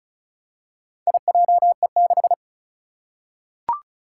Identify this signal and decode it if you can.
Morse code, heard as a keyed tone, about 36 wpm, 700 Hz. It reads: IJE6
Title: Four Challenging Letter-Number Combinations 35wpm